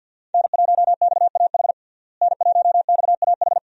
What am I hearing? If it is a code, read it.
Morse code, heard as a keyed tone, about 47 wpm, 700 Hz. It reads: N1XAH N1XAH
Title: USA Call Signs - 5 Characters - Repeated 2x - 50wpm